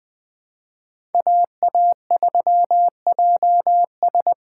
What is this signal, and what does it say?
Morse code, heard as a keyed tone, about 20 wpm, 700 Hz. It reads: AA3JS
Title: USA Call Signs - 5 Characters - 4x Character Spacing - 20wpm